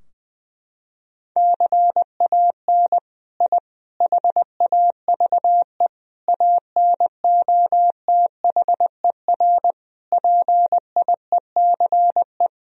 Morse code, heard as a keyed tone, about 20 wpm, 700 Hz. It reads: CAN I HAVE ANOTHER PIECE
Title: Sentences from Top 1000 Words - Review Part 2 - 20wpm